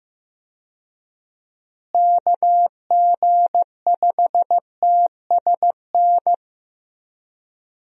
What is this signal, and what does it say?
Morse code, heard as a keyed tone, about 15 wpm, 700 Hz. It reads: KG5TSN
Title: USA Call Signs - 6 Characters - Repeated 2x - 15wpm